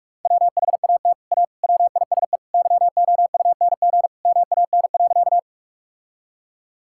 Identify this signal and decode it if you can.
Morse code, heard as a keyed tone, about 41 wpm, 700 Hz. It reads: WHAT A WISE YOUNG MAN.
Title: Sentences from Top 1600 Words 45wpm